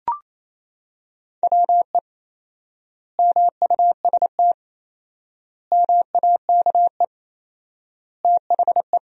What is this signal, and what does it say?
Morse code, heard as a keyed tone, about 28 wpm, 700 Hz. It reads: WE MUST MAKE THE
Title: 4x Word-Spacing - Sentences from Top 400 Words - 28wpm